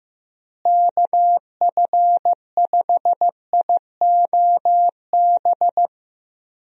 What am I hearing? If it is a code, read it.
Morse code, heard as a keyed tone, about 15 wpm, 700 Hz. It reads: KF5IOB
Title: USA Call Signs - 6 Characters - Repeated 2x - 15wpm